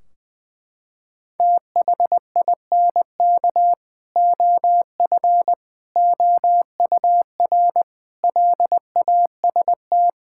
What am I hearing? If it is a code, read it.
Morse code, heard as a keyed tone, about 20 wpm, 700 Hz. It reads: THINK OF OUR LAST
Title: Sentences from Top 1600 Words 20wpm